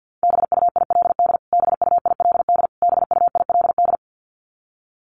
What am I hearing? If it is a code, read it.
Morse code, heard as a keyed tone, about 50 wpm, 700 Hz. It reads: BUILD BUILD BUILD
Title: Top 500 Words Repeated 50wpm